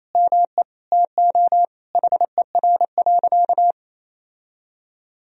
Morse code, heard as a keyed tone, about 28 wpm, 700 Hz. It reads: ME TO HER.
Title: Sentences from Top 1000 Words - Review Part 2 - 28wpm